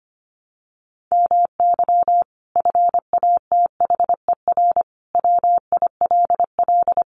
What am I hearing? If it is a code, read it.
Morse code, heard as a keyed tone, about 25 wpm, 700 Hz. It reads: MY FATHER WILL
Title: Sentences from Top 1100 Words 25wpm